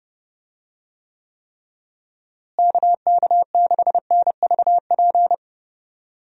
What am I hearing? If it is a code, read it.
Morse code, heard as a keyed tone, about 30 wpm, 700 Hz. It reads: KK6NVP
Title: USA Call Signs - 6 Characters - Repeated 3x - 30wpm